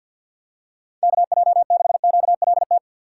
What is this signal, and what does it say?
Morse code, heard as a keyed tone, about 52 wpm, 700 Hz. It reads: KJ6QLT
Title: USA Call Signs - 6 Characters - 3x Character Spacing - 50wpm